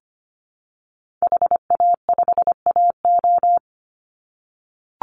Morse code, heard as a keyed tone, about 25 wpm, 700 Hz. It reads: HA5AO
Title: Call Signs - 25wpm